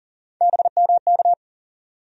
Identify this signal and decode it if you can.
Morse code, heard as a keyed tone, about 40 wpm, 700 Hz. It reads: BMK
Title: USA Call Sign Postfixes - 3 Letters 40wpm